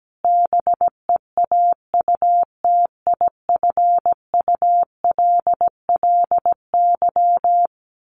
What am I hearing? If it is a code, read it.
Morse code, heard as a keyed tone, about 17 wpm, 700 Hz. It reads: BEAUTIFULLY
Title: A Sound Of Thunder 17wpm